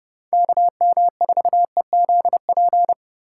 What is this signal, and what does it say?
Morse code, heard as a keyed tone, about 34 wpm, 700 Hz. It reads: KM4EZP